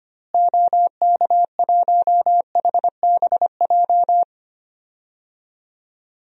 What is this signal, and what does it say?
Morse code, heard as a keyed tone, about 25 wpm, 700 Hz. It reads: OK1HBJ
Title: DXCC-100 Call Signs - 6 Characters - Repeated 3x - 25wpm